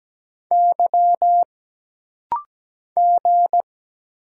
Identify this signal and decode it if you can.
Morse code, heard as a keyed tone, about 17 wpm, 700 Hz. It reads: Y G
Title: Single Letters - Rapid-Fire 17wpm